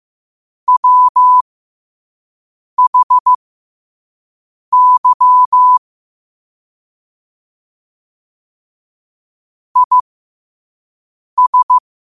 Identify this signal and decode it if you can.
Morse code, heard as a keyed tone, about 15 wpm, spaced as for 5 wpm, 1000 Hz. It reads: WHY IS
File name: qotdfiles / 5wpm